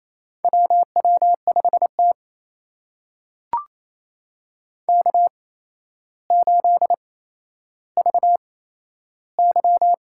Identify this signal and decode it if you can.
Morse code, heard as a keyed tone, about 28 wpm, spaced as for 12 wpm, 700 Hz. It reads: WW5T K8VY
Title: USA Call Signs - 4 Characters - 8x Character Spacing - 28wpm